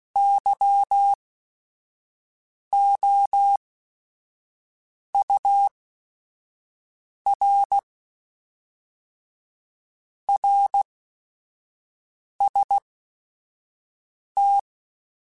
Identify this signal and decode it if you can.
Morse code, heard as a keyed tone, about 16 wpm, spaced as for 5 wpm, 800 Hz. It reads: YOUR RST
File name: RSTs / RST-mp3s / Your RSTs @5